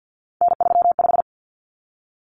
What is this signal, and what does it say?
Morse code, heard as a keyed tone, about 50 wpm, 700 Hz. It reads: N35
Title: Common Cut Numbers - 3-Digit 50wpm